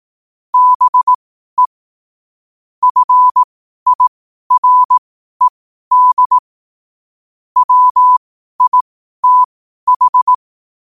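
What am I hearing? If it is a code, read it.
Morse code, heard as a keyed tone, about 18 wpm, spaced as for 12 wpm, 1000 Hz. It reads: BE FIRED WITH
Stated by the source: Morse code at 12 wpm.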